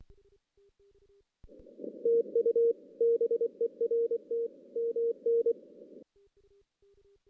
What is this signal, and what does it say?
Morse code, heard as a keyed tone, about 24 wpm, 460 Hz. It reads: TU BERT MN
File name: 2016 JAN CW NAQP / 2016 Jan naqp k5ot - 0-6